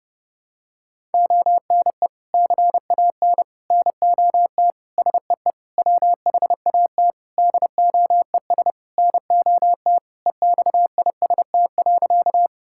Morse code, heard as a keyed tone, about 30 wpm, 700 Hz. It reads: ONE CAN NOT SEE WHAT DOES NOT EXIST.
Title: Sentences from Top 1000 Words - Review Part 5 - 30wpm